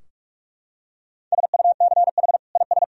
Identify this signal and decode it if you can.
Morse code, heard as a keyed tone, about 46 wpm, 700 Hz. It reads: SUCH IS
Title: Sentences from Top 1000 Words - Review Part 2 - 45wpm